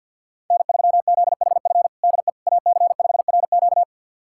Morse code, heard as a keyed tone, about 50 wpm, 690 Hz. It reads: N3ZHF DE AC5RQ
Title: USA Callsign DE Callsign - 5 Characters - 50wpm